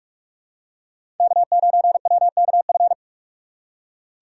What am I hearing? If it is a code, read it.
Morse code, heard as a keyed tone, about 46 wpm, 700 Hz. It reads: K9WKF
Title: USA Call Signs - 5 Characters - 3x Character Spacing - 45wpm